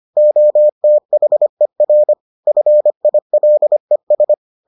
Morse code, heard as a keyed tone, about 25 wpm, 600 Hz. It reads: OTHER FILES